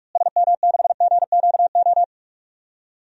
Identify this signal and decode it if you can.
Morse code, heard as a keyed tone, about 46 wpm, 700 Hz. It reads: SM6GQO
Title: DXCC-100 Call Signs - 6 Characters - Repeated 2x - 45wpm